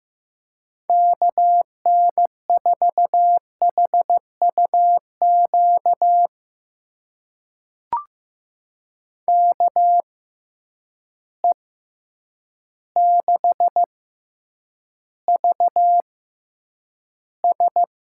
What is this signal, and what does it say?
Morse code, heard as a keyed tone, about 15 wpm, spaced as for 8 wpm, 700 Hz. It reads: KN4HUQ KE6VS
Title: USA Call Signs - 6 Characters - 6x Character Spacing - 15wpm